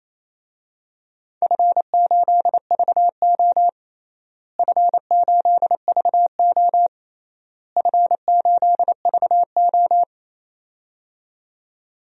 Morse code, heard as a keyed tone, about 28 wpm, 700 Hz. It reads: F8VO F8VO F8VO
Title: DXCC-100 Call Signs - 4 Characters - Repeated 3x - 28wpm